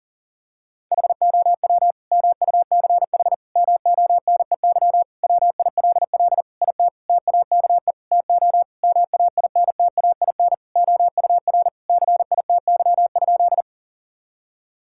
Morse code, heard as a keyed tone, about 40 wpm, 700 Hz. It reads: HOW MUCH MONEY WILL IT TAKE TO MAINTAIN OUR CITY?
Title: Sentences from Top 1000 Words - Review Part 5 - 40wpm